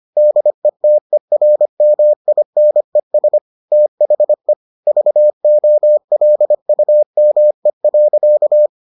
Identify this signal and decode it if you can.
Morse code, heard as a keyed tone, about 25 wpm, 600 Hz. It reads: DETERMINES THE VOLUME.